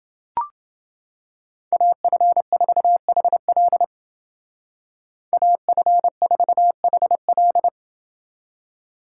Morse code, heard as a keyed tone, about 28 wpm, 700 Hz. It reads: AF4HL AF4HL